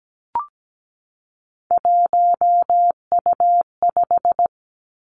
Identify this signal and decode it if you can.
Morse code, heard as a keyed tone, about 17 wpm, 700 Hz. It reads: 1U5
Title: Three Challenging Letter-Number Combinations 17wpm